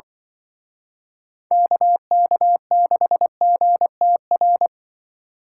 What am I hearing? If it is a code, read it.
Morse code, heard as a keyed tone, about 24 wpm, 700 Hz. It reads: KK6GTR
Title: Speed Racing - USA Call Signs 20wpm